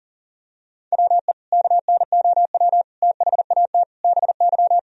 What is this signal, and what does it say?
Morse code, heard as a keyed tone, about 40 wpm, 700 Hz. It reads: WE KNOW THAT BY
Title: Sentences from Top 1000 Words - Review Part 3 - 40wpm